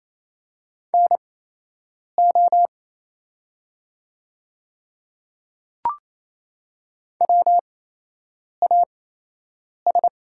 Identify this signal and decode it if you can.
Morse code, heard as a keyed tone, about 28 wpm, spaced as for 8 wpm, 700 Hz. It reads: NO WAS
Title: 8x Character Spacing - Top 200 Words - 28wpm